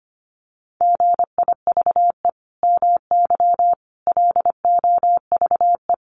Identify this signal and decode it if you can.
Morse code, heard as a keyed tone, about 25 wpm, 700 Hz. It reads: GIVE MY LOVE
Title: Sentences from Top 1400 Words 25wpm